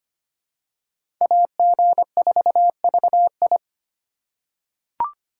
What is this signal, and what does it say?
Morse code, heard as a keyed tone, about 25 wpm, 700 Hz. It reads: AG4VI
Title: USA Call Signs - 5 Characters - 3x Character Spacing - 25wpm